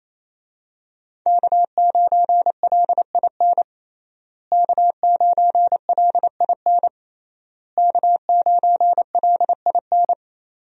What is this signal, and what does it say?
Morse code, heard as a keyed tone, about 28 wpm, 700 Hz. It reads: K9LIN K9LIN K9LIN
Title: USA Call Signs - 5 Characters - Repeated 3x - 28wpm